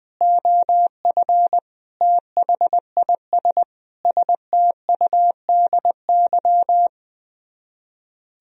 Morse code, heard as a keyed tone, about 20 wpm, 700 Hz. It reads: OF THIS STUDY
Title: Speed-Racing - Sets of 5 Words 20wpm